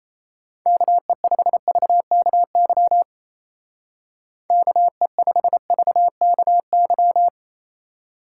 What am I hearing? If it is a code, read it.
Morse code, heard as a keyed tone, about 33 wpm, 700 Hz. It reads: KE5VKY KE5VKY
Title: Speed Racing - USA 2x3 Call Signs 28wpm